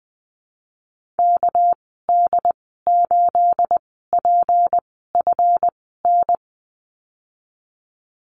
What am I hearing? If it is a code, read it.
Morse code, heard as a keyed tone, about 20 wpm, spaced as for 14 wpm, 700 Hz. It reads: KD8PFN
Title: 219 Lesson 35g - Number 0 - Call Signs 20wpm